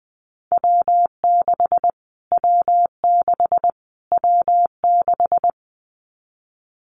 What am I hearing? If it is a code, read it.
Morse code, heard as a keyed tone, about 20 wpm, 700 Hz. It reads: W6 W6 W6
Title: Callsign Prefixes for Top 25 DXCC - Repeated 3x - 20wpm